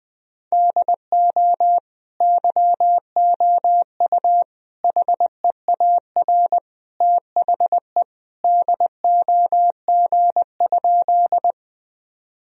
Morse code, heard as a keyed tone, about 21 wpm, 700 Hz. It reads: DO YOU HEAR THE DOG?
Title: Sentences from Top 1000 Words - Review Part 2 - 20wpm